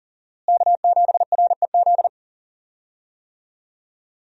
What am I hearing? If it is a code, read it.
Morse code, heard as a keyed tone, about 40 wpm, 700 Hz. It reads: K7REZ
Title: USA Call Signs - 5 Characters - Repeated 2x - 40wpm